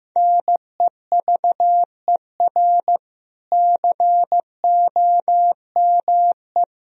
Morse code, heard as a keyed tone, about 15 wpm, 700 Hz. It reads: NEVER COME
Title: Sentences from Top 1000 Words - Review Part 1 - 15wpm